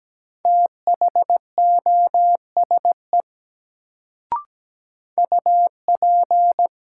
Morse code, heard as a keyed tone, about 17 wpm, 700 Hz. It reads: THOSE UP
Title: Sets of 2 Words - New Word Drill - 17wpm